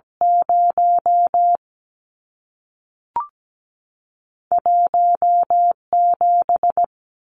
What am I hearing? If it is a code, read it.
Morse code, heard as a keyed tone, about 17 wpm, 700 Hz. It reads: 0 1 7
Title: Numbers - 2-Digit 17wpm